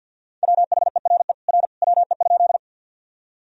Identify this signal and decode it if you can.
Morse code, heard as a keyed tone, about 45 wpm, 700 Hz. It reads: WHERE R WE?
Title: Sentences from Top 200 Words and Abbreviations 50wpm